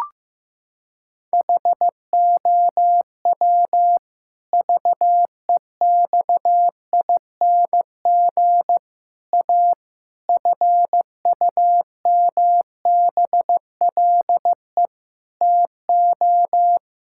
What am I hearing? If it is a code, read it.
Morse code, heard as a keyed tone, about 16 wpm, 700 Hz. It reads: HOW VEXING A FUMBLE TO
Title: Pangram Challenge v2 - 15wpm